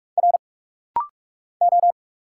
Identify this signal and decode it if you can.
Morse code, heard as a keyed tone, about 44 wpm, 700 Hz. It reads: R O
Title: Single Letters - Rapid-Fire 45wpm